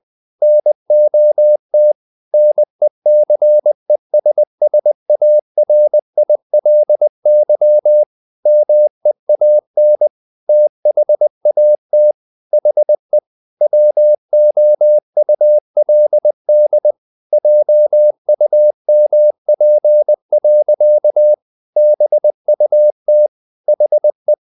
Morse code, heard as a keyed tone, about 20 wpm, 600 Hz. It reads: NOT NECESSARILY MEAN THAT HE WOULD JUMP. BUT HE
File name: the old man and the sea / the old man and the sea part 1 20wpm